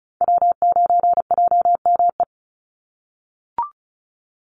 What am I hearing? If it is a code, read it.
Morse code, heard as a keyed tone, about 34 wpm, 700 Hz. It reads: W9JME
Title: USA Call Signs 35wpm